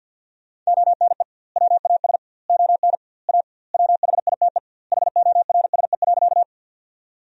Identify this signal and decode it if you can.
Morse code, heard as a keyed tone, about 50 wpm, 700 Hz. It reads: ONE WAS ON A WHITE HORSE.
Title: 2x Word-Spacing - Sentences from Top 300 Words - 50wpm